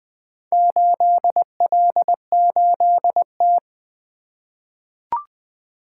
Morse code, heard as a keyed tone, about 20 wpm, 700 Hz. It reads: 8L8T
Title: Four Letter-Number Combinations - 20wpm